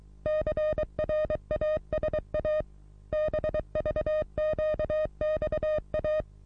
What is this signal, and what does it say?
Morse code, heard as a keyed tone, about 24 wpm, 620 Hz. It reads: CRASA BVQXA
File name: repos / Morsekurs / DJ4UF Morsen27